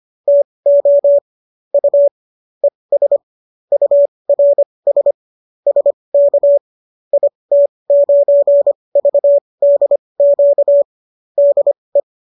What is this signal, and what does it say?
Morse code, heard as a keyed tone, about 25 wpm, spaced as for 20 wpm, 570 Hz. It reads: TO U ES URS SK IT9VDQ DE